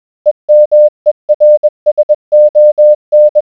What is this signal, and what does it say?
Morse code, heard as a keyed tone, about 20 wpm, 600 Hz. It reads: EMERSON